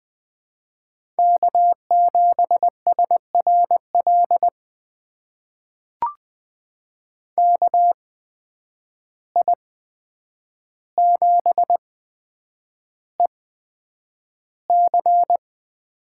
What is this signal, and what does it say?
Morse code, heard as a keyed tone, about 20 wpm, spaced as for 9 wpm, 700 Hz. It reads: K7SRL KI7EC
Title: USA Call Signs - 5 Characters - 8x Character Spacing - 20wpm